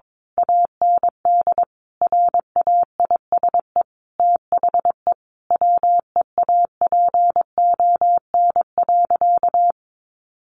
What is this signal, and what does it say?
Morse code, heard as a keyed tone, about 22 wpm, 700 Hz. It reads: AND RAISE THE WEAPON.
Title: A Sound Of Thunder 22wpm